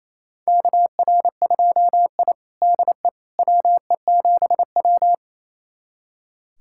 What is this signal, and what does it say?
Morse code, heard as a keyed tone, about 28 wpm, 700 Hz. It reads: KR2I DE WE7W
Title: USA Callsign DE Callsign - 4 Characters - 28wpm